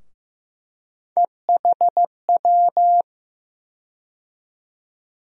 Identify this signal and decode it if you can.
Morse code, heard as a keyed tone, about 15 wpm, 700 Hz. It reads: EHW
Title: USA Call Sign Postfixes - 3 Letters 15wpm